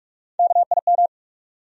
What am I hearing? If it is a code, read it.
Morse code, heard as a keyed tone, about 43 wpm, 700 Hz. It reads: KIM
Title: USA Common Names - Max 4 Letters - 45wpm